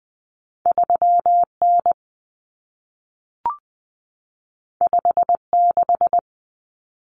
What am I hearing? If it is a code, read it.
Morse code, heard as a keyed tone, about 20 wpm, 700 Hz. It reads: 3N 56
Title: Common Cut Numbers - 2-Digit 20wpm